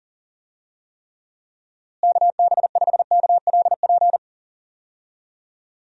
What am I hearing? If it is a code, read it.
Morse code, heard as a keyed tone, about 40 wpm, 700 Hz. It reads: KB5KLP
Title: USA Call Signs - 6 Characters - Repeated 2x - 40wpm